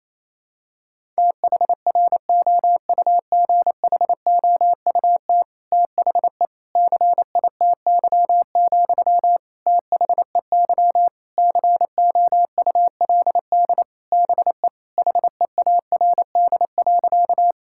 Morse code, heard as a keyed tone, about 28 wpm, 700 Hz. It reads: THROUGHOUT THE CITY, THEY COULD BE HEARD.
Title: Sentences from Top 1000 Words - Review Part 5 - 28wpm